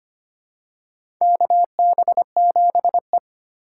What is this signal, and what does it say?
Morse code, heard as a keyed tone, about 25 wpm, 700 Hz. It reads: KB7E
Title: USA Call Signs - 6x Character Spacing - 25wpm